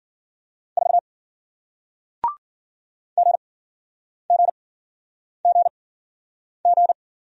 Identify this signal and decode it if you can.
Morse code, heard as a keyed tone, about 55 wpm, spaced as for 9 wpm, 700 Hz. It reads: 4 GGGG